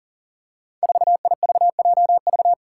Morse code, heard as a keyed tone, about 38 wpm, 700 Hz. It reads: 4IVJV
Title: Five Challenging Letter-Number Combinations 40wpm